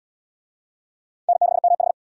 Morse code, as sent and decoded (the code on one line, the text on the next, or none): -. -.... -- ....
N6MH